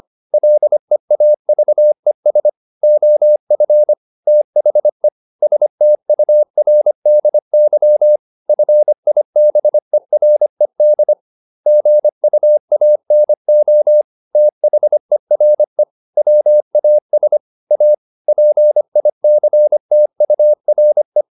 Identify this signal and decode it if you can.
Morse code, heard as a keyed tone, about 25 wpm, 600 Hz. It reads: LEAVES OF THE STURDY FIBERED GUANO THERE WAS A PICTURE
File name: the old man and the sea part 1 25wpm